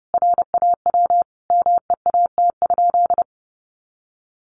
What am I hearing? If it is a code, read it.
Morse code, heard as a keyed tone, about 30 wpm, 700 Hz. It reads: RAW MEAT?
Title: Sentences from Top 1500 Words 30wpm